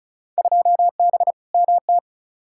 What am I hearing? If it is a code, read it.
Morse code, heard as a keyed tone, about 34 wpm, 700 Hz. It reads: 2B MT
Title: Field Day Exchanges Encore - 35wpm